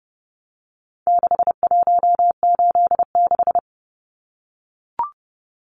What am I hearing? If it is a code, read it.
Morse code, heard as a keyed tone, about 30 wpm, 700 Hz. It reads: 6186
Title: Numbers - 4-Digit 30wpm